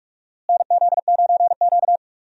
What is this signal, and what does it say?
Morse code, heard as a keyed tone, about 41 wpm, 700 Hz. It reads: NZ9Q
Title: USA Call Signs - 4 Characters - Repeated 3x - 45wpm